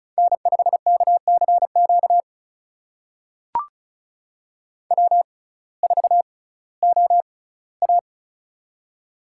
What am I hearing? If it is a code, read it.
Morse code, heard as a keyed tone, about 36 wpm, spaced as for 19 wpm, 700 Hz. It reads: N5KCQ W4OA
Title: USA Call Signs - 6x Character Spacing - 35wpm